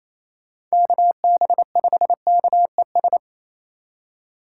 Morse code, heard as a keyed tone, about 34 wpm, 700 Hz. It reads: KB5KES
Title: Speed Racing - USA 2x3 Call Signs 28wpm